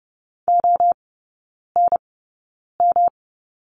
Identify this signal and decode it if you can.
Morse code, heard as a keyed tone, about 30 wpm, spaced as for 9 wpm, 700 Hz. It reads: ONM
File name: Lesson 18e - Letter m - Three Characters 30wpm